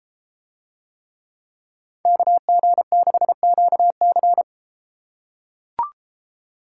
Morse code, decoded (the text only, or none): KG6QC